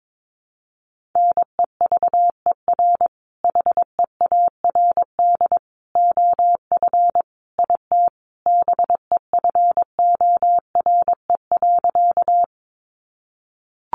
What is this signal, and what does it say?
Morse code, heard as a keyed tone, about 22 wpm, 700 Hz. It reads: NEVER HEARD OF IT BEFORE.
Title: Sentences from Top 400 Words 22wpm